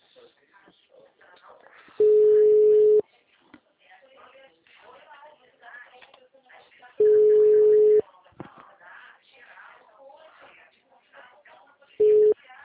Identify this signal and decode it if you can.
Ringback tone (Europe, 425 Hz), with speech.
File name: out- -2003- - - .57460